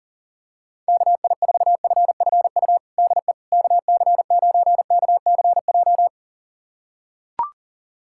Morse code, heard as a keyed tone, about 40 wpm, 700 Hz. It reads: KI4FFU DE KC9KCJ
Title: USA Callsign DE Callsign 40wpm